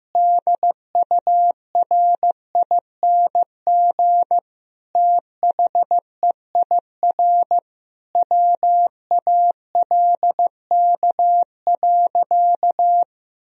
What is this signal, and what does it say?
Morse code, heard as a keyed tone, about 16 wpm, 700 Hz. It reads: DURING THEIR WALK.
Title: Sentences from Top 1000 Words - Review Part 2 - 15wpm